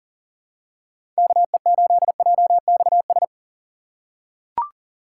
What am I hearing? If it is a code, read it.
Morse code, heard as a keyed tone, about 40 wpm, 700 Hz. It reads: KE8JXS
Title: USA Call Signs - 6 Characters - 6x Character Spacing - 40wpm